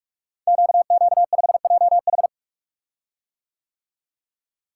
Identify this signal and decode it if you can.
Morse code, heard as a keyed tone, about 44 wpm, 690 Hz. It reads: QQ5JH
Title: Code Groups - 5 Letter:Numbers - 45wpm